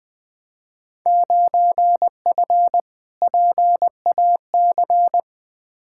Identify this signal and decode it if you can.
Morse code, heard as a keyed tone, about 21 wpm, 700 Hz. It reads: 9F PAC
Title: Field Day Exchanges Encore - 20wpm